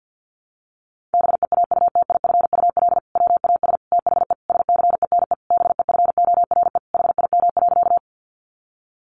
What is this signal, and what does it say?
Morse code, heard as a keyed tone, about 50 wpm, 700 Hz. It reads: BEAUTIFUL WAS THE SCENE BEFORE HIM.
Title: Sentences from Top 1100 Words 50wpm